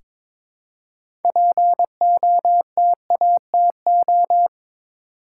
Morse code, heard as a keyed tone, about 22 wpm, 700 Hz. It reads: POTATO
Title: Six Letter Words - 22wpm